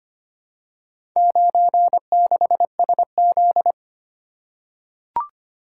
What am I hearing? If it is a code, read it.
Morse code, heard as a keyed tone, about 25 wpm, 700 Hz. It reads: 96SZ
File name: practice / 25wpm / Four Letter-Number Combinations - 25wpm